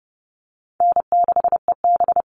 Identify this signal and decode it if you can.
Morse code, heard as a keyed tone, about 30 wpm, 700 Hz. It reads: N6EB
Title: All Cut Numbers - 4-Digit 30wpm